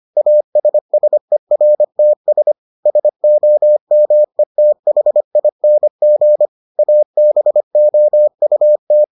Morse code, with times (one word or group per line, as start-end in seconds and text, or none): 0.2-2.5 ASSERTS
2.9-6.5 SOMETHING
6.8-9.0 ABOUT